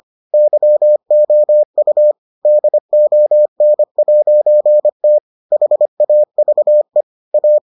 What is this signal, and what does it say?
Morse code, heard as a keyed tone, about 25 wpm, 600 Hz. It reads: YOU DON'T HAVE A